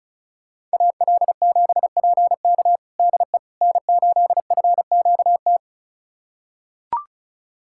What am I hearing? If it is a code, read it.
Morse code, heard as a keyed tone, about 36 wpm, 700 Hz. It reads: AL7PK DE N8FQT
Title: USA Callsign DE Callsign - 5 Characters - 35wpm